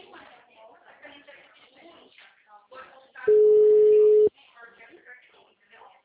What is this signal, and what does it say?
Ringback tone (Europe, 425 Hz), with speech.
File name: out- -2023- - - .42606